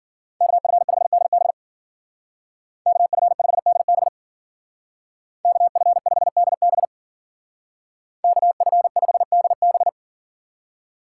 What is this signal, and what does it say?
Morse code, heard as a keyed tone, about 49 wpm, 700 Hz. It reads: KF5DB KF5DB KF5DB KF5DB
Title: Speed Racing - USA 2x2 Call Signs 40wpm